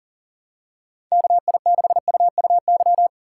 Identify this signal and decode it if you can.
Morse code, heard as a keyed tone, about 39 wpm, 700 Hz. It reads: KI6UUY
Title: USA Call Signs - 6 Characters - 4x Character Spacing - 40wpm